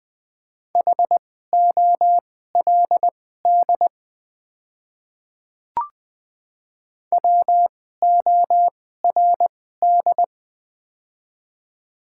Morse code, heard as a keyed tone, about 20 wpm, spaced as for 15 wpm, 690 Hz. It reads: HOLD WORD
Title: 2x Character Spacing - Top 500 Words - 20wpm